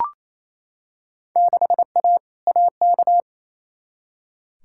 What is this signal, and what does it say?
Morse code, heard as a keyed tone, about 28 wpm, 700 Hz. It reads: 6A AK
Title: Field Day Exchanges Encore - 28wpm